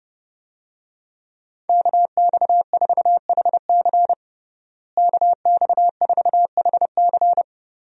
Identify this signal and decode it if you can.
Morse code, heard as a keyed tone, about 29 wpm, 700 Hz. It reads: KX4HC KX4HC
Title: USA Call Signs - 5 Characters - Repeated 2x - 30wpm